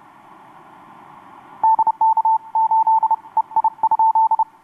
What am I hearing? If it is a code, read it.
Morse code, heard as a keyed tone, about 29 wpm, 890 Hz. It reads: DK8 EI?
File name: xw2cw rx6am etc 12m cw 26nov2012 1153